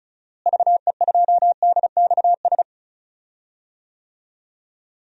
Morse code, heard as a keyed tone, about 34 wpm, 700 Hz. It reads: VE2DXS
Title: DXCC-100 Call Signs - 6 Characters - Repeated 3x - 35wpm